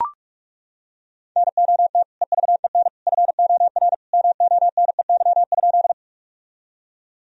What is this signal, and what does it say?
Morse code, heard as a keyed tone, about 41 wpm, 700 Hz. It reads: NOT EVEN FOR MONEY?
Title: Sentences from Top 1000 Words - Review Part 1 - 45wpm